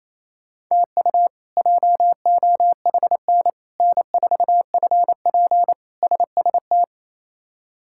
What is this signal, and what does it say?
Morse code, heard as a keyed tone, about 28 wpm, 700 Hz. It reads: TU JOHN N4FP SST
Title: K1USN SST QSO Practice - 28wpm